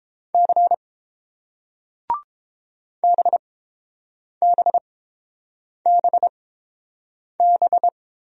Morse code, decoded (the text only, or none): C BBBB